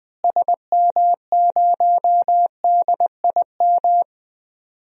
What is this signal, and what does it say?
Morse code, heard as a keyed tone, about 20 wpm, 700 Hz. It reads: SM0DIM
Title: DXCC-100 Call Signs - 6 Characters - Repeated 2x - 20wpm